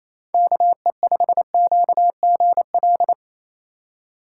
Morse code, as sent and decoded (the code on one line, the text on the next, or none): -.- . ..... --.- --. .-..
KE5QGL